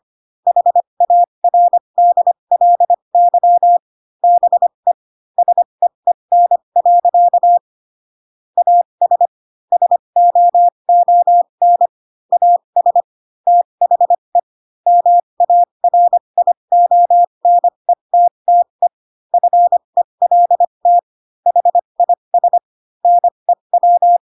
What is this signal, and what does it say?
Morse code, heard as a keyed tone, about 25 wpm, spaced as for 22 wpm, 700 Hz. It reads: HARDLY BE SEEN. AS SOON AS THE MARIONETTE FELT HIS NEW